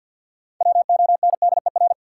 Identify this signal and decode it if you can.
Morse code, heard as a keyed tone, about 53 wpm, 700 Hz. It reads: WONDER
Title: Sentences from Top 500 Words - New Word Drill - 50wpm